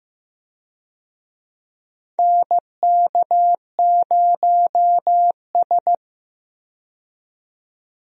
Morse code, heard as a keyed tone, about 15 wpm, 700 Hz. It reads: NK0S
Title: USA Call Signs - 4 Characters - 15wpm